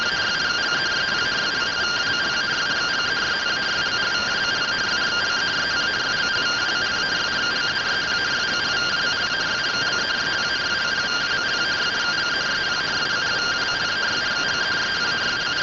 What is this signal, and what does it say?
Two-tone teleprinter signal, about 170 Hz shift.
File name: findings / AXIAL fr45 3mm.gen-14 rtty rx